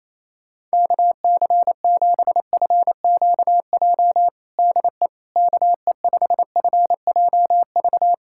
Morse code, heard as a keyed tone, about 27 wpm, 700 Hz. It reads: KC7FQJ DE KE5FJV
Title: USA Callsign DE Callsign - 6 Characters - 28wpm